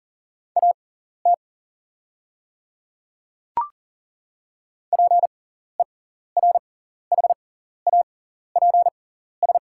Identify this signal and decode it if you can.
Morse code, heard as a keyed tone, about 40 wpm, spaced as for 14 wpm, 690 Hz. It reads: AT PERHAPS
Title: 6x Character Spacing - Top 500 Words - 40wpm